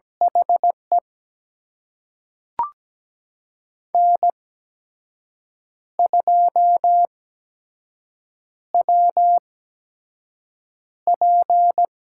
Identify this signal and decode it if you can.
Morse code, heard as a keyed tone, about 17 wpm, spaced as for 6 wpm, 700 Hz. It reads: HE N2WP
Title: USA Call Signs - 4 Characters - 8x Character Spacing - 17wpm